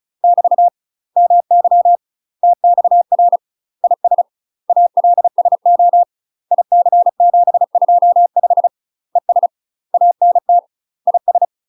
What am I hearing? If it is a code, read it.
Morse code, heard as a keyed tone, about 36 wpm, 700 Hz. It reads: <BT> MY TXR IS ALSO IC725 ES ANT IS
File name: UK MASTER QSOs TRAINING / 35WPM / UK MASTER QSOs 0026